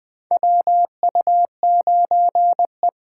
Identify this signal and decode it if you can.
Morse code, heard as a keyed tone, about 20 wpm, 700 Hz. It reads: WU9E
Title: USA Callsign DE Callsign - 4 Characters - 20wpm